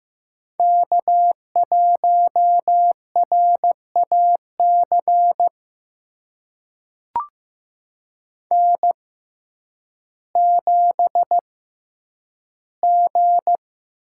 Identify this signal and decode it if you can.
Morse code, heard as a keyed tone, about 15 wpm, spaced as for 9 wpm, 700 Hz. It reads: K1RAC N7G
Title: USA Call Signs - 5 Characters - 6x Character Spacing - 15wpm